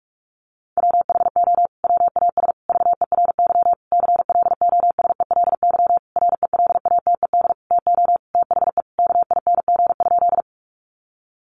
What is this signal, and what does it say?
Morse code, heard as a keyed tone, about 44 wpm, 700 Hz. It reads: WHO WAS VERY CLOSELY RELATED TO THE KING?
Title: Sentences from Top 1400 Words 45wpm